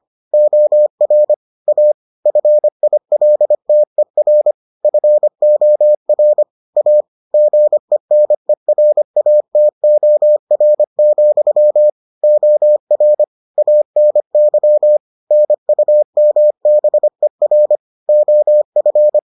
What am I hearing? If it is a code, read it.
Morse code, heard as a keyed tone, about 25 wpm, 600 Hz. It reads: OR A FILTER FOR A GENERATOR, OR ANY NUMBER OF